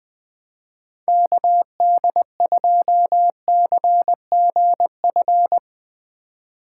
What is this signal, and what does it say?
Morse code, heard as a keyed tone, about 21 wpm, 700 Hz. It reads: KD2CGF